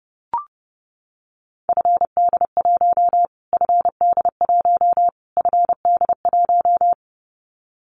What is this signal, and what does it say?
Morse code, heard as a keyed tone, about 30 wpm, 700 Hz. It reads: FD1 FD1 FD1
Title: Callsign Prefixes for Top 25 DXCC - Repeated 3x - 30wpm